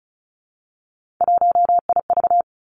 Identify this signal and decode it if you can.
Morse code, heard as a keyed tone, about 36 wpm, 700 Hz. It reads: 1IV
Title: Three Challenging Letter-Number Combinations 35wpm